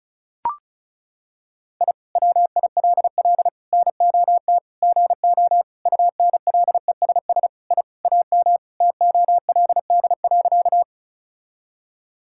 Morse code, heard as a keyed tone, about 36 wpm, 700 Hz. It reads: I WILL NOT GO UNLESS I AM TOLD.
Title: Sentences from Top 1000 Words - Review Part 6 - 35wpm